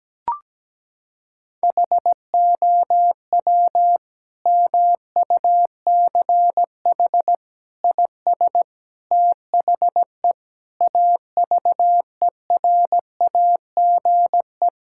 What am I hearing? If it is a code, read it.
Morse code, heard as a keyed tone, about 17 wpm, 700 Hz. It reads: HOW MUCH IS THE AVERAGE
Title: Sentences from Top 1000 Words - Review Part 6 - 17wpm